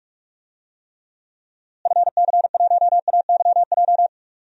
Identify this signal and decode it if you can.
Morse code, heard as a keyed tone, about 41 wpm, 700 Hz. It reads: FC1AYJ
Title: DXCC-100 Call Signs - 6 Characters - Repeated 3x - 45wpm